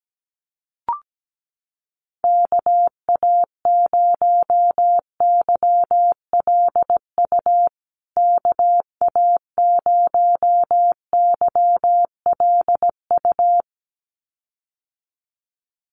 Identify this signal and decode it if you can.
Morse code, heard as a keyed tone, about 17 wpm, 700 Hz. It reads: KA0YLU KA0YLU
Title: USA Call Signs Repeated 2x 17wpm